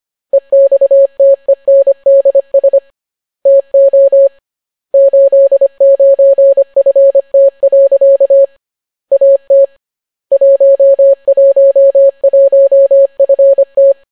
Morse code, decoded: EXTENDS TO 89FT. AT 111FT